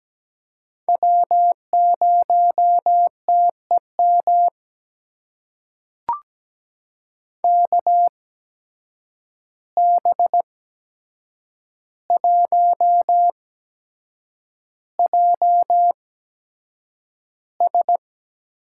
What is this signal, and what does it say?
Morse code, heard as a keyed tone, about 17 wpm, spaced as for 7 wpm, 700 Hz. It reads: W0TEM KB1JS
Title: USA Call Signs - 5 Characters - 8x Character Spacing - 17wpm